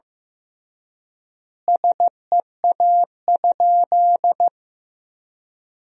Morse code, heard as a keyed tone, about 16 wpm, 700 Hz. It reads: SEA?
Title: 3x Word-Spacing - Sentences from Top 500 Words - 15wpm